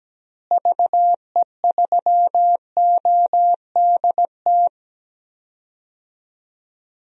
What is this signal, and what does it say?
Morse code, heard as a keyed tone, about 17 wpm, 700 Hz. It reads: VE3ODT
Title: DXCC-100 Call Signs - 6 Characters - Repeated 3x - 17wpm